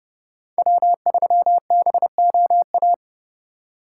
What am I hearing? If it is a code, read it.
Morse code, heard as a keyed tone, about 28 wpm, 700 Hz. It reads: W3BOA